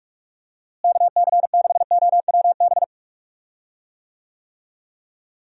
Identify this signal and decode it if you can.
Morse code, heard as a keyed tone, about 46 wpm, 700 Hz. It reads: KC6OWB
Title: USA Call Signs - 6 Characters - Repeated 3x - 45wpm